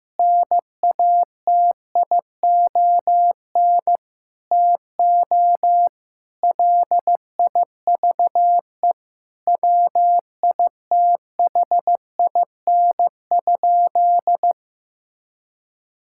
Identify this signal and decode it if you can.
Morse code, heard as a keyed tone, about 15 wpm, 700 Hz. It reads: NATION TO LIVE WITHIN?
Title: Sentences from Top 1000 Words - Review Part 6 - 15wpm